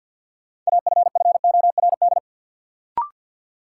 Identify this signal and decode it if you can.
Morse code, heard as a keyed tone, about 47 wpm, 700 Hz. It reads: AFFORD
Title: Sentences from Top 1100 Words - New Word Drill - 50wpm